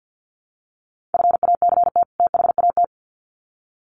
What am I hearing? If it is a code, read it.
Morse code, heard as a keyed tone, about 50 wpm, 700 Hz. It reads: FACT THAT
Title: Sets of 2 Words 50wpm